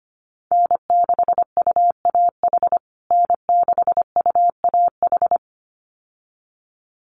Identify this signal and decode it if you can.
Morse code, heard as a keyed tone, about 25 wpm, 700 Hz. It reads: N6UAH N6UAH
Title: USA Call Signs Repeated 2x 25wpm